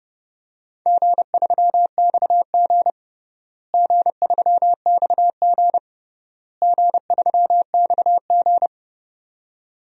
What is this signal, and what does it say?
Morse code, heard as a keyed tone, about 30 wpm, 700 Hz. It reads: G3XG G3XG G3XG
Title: DXCC-100 Call Signs - 4 Characters - Repeated 3x - 30wpm